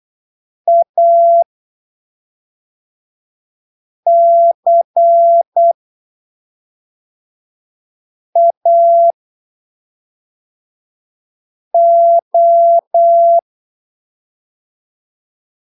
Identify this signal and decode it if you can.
Morse code, heard as a keyed tone, about 8 wpm, spaced as for 3 wpm, 680 Hz. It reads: ACAO
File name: arquivos / texto / texto 30 08ppm